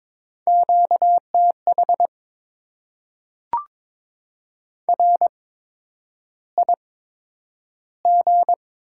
Morse code, heard as a keyed tone, about 22 wpm, spaced as for 9 wpm, 700 Hz. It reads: QTH RIG
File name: Top 20 QSO Elements - 8x Character Spacing - 22wpm